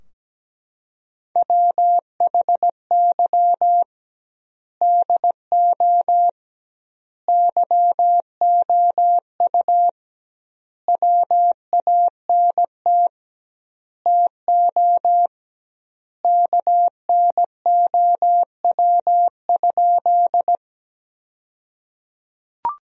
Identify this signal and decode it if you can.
Morse code, heard as a keyed tone, about 17 wpm, 700 Hz. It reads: WHY DO YOU WANT TO KNOW?
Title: 2x Word-Spacing - Sentences from Top 200 Words - 17wpm